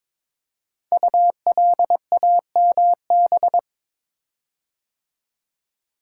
Morse code, heard as a keyed tone, about 22 wpm, 700 Hz. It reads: ULAMB
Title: Code Groups - 5 Letters - 22wpm